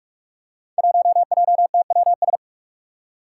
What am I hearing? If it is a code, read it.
Morse code, heard as a keyed tone, about 46 wpm, 700 Hz. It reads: 1JTWS
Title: Five Challenging Letter-Number Combinations 45wpm